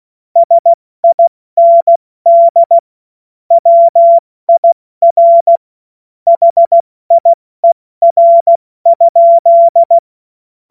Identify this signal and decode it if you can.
Morse code, heard as a keyed tone, about 16 wpm, 680 Hz. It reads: SIND WIR HIER?
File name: Morsekurs / Witze / Witz 16. August 70 F80